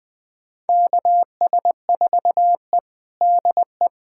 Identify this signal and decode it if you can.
Morse code, heard as a keyed tone, about 20 wpm, 700 Hz. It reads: KS4E DE
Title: USA Callsign DE Callsign - 4 Characters - 20wpm